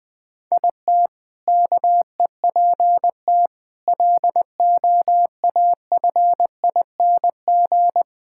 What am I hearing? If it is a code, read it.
Morse code, heard as a keyed tone, about 20 wpm, 700 Hz. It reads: IT KEPT LOAFING
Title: All-Age Appropriate Jokes - 20wpm